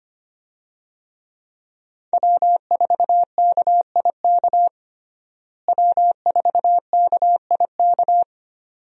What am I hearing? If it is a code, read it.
Morse code, heard as a keyed tone, about 25 wpm, 700 Hz. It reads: W4KIK W4KIK
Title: USA Call Signs - 5 Characters - Repeated 2x - 25wpm